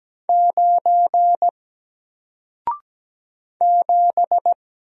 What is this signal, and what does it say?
Morse code, heard as a keyed tone, about 17 wpm, 700 Hz. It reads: 9 7
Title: Single Letter-Number Alt 17wpm